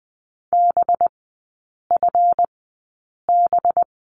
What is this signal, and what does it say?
Morse code, heard as a keyed tone, about 20 wpm, spaced as for 8 wpm, 700 Hz. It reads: BFB
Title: 184 Lesson 30e - Letter b - Three Characters 20wpm